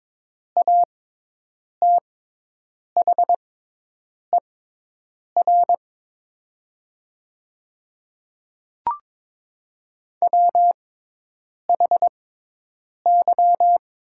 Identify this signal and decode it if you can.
Morse code, heard as a keyed tone, about 22 wpm, spaced as for 8 wpm, 700 Hz. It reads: ATHER WHY
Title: 6x Character Spacing - Top 300 Words - 22wpm